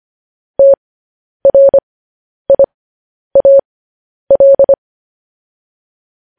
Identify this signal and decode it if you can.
Morse code, heard as a keyed tone, about 25 wpm, spaced as for 10 wpm, 550 Hz. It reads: TRIAL